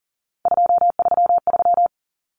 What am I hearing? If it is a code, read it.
Morse code, heard as a keyed tone, about 40 wpm, 700 Hz. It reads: 233
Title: Numbers - 3-Digit 40wpm